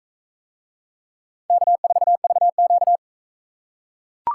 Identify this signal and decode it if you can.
Morse code, heard as a keyed tone, about 44 wpm, 700 Hz. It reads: K4VQ